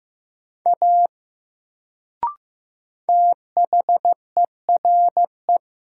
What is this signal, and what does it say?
Morse code, heard as a keyed tone, about 15 wpm, 700 Hz. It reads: A THERE
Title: Sentences from Top 100 Words - New Word Drill - 15wpm